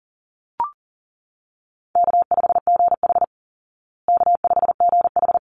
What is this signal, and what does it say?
Morse code, heard as a keyed tone, about 40 wpm, 700 Hz. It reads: K5GH K5GH
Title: Easy Call Signs Repeated 2x - 40wpm